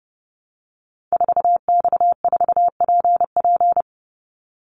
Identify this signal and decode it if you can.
Morse code, heard as a keyed tone, about 30 wpm, 700 Hz. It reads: 4X4PP
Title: Call Signs - 30wpm